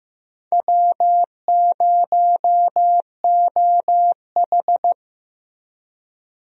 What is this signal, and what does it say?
Morse code, heard as a keyed tone, about 16 wpm, 700 Hz. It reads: W0OH
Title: USA Call Signs - 4 Characters - Repeated 3x - 15wpm